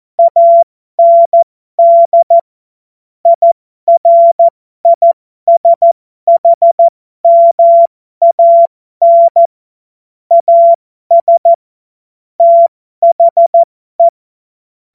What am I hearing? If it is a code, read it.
Morse code, heard as a keyed tone, about 14 wpm, spaced as for 12 wpm, 680 Hz. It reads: AND IRISHMAN AS THE
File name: Morsekurs / Witze